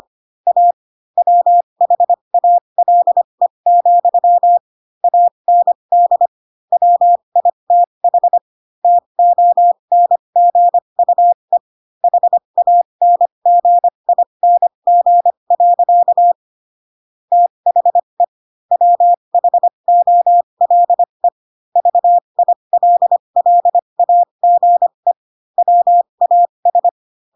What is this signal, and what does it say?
Morse code, heard as a keyed tone, about 25 wpm, spaced as for 22 wpm, 700 Hz. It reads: A WHALE, AND WITH TONGUE HANGING. THE WHOLE VILLAGE WAS
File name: Pinocchio / Pinocchio ch6 18 25 ik1hge